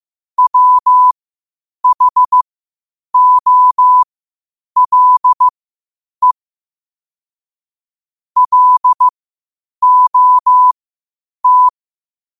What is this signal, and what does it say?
Morse code, heard as a keyed tone, about 15 wpm, spaced as for 8 wpm, 1000 Hz. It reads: WHOLE LOT
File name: 8wpm